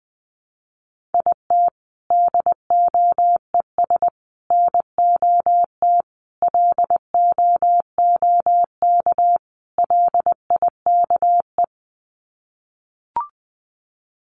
Morse code, heard as a keyed tone, about 19 wpm, 700 Hz. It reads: IT DOES NOT LOOK LIKE
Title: Sets of 5 Words Encore2 20wpm